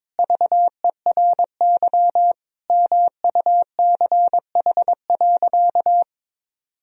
Morse code, heard as a keyed tone, about 22 wpm, 700 Hz. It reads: VERY MUCH.
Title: Sentences from Top 1000 Words - Review Part 3 - 22wpm